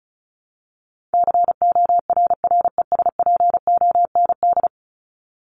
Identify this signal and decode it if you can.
Morse code, heard as a keyed tone, about 34 wpm, 700 Hz. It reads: CORRESPOND
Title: ING Suffix - Top 100 Words with Base Words 35wpm